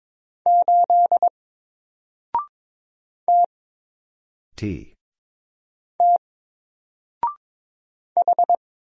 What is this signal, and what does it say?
Morse code, heard as a keyed tone, about 22 wpm, 700 Hz. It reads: 8 T T H
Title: Single Character (Letter:Number:Punctuation:Prosign) Alt 22wpm